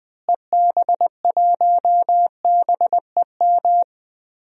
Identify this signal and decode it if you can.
Morse code, heard as a keyed tone, about 20 wpm, 700 Hz. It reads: EB1BEM
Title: DXCC-100 Call Signs - 6 Characters - Repeated 3x - 20wpm